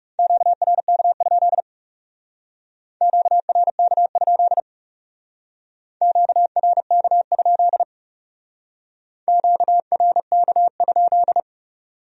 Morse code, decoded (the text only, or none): QRK? QRK? QRK? QRK?